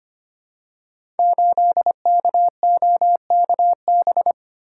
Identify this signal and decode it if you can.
Morse code, heard as a keyed tone, about 25 wpm, 700 Hz. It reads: 8KOKB
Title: Code Groups - 5 Letter:Numbers - 25wpm